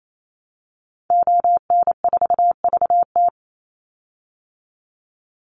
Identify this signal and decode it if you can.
Morse code, heard as a keyed tone, about 28 wpm, 700 Hz. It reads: ON4VT
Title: Call Signs Repeated 3x - 28wpm